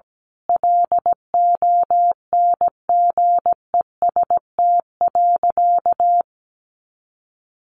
Morse code, heard as a keyed tone, about 17 wpm, 700 Hz. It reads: LONGEST.
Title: A Sound Of Thunder 17wpm